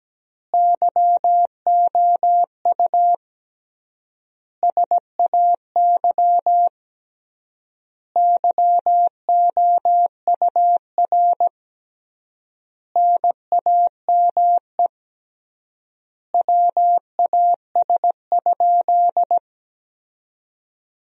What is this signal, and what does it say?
Morse code, heard as a keyed tone, about 17 wpm, 700 Hz. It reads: YOU SAY YOUR NAME WAS?
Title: 3x Word-Spacing - Sentences from Top 200 Words - 17wpm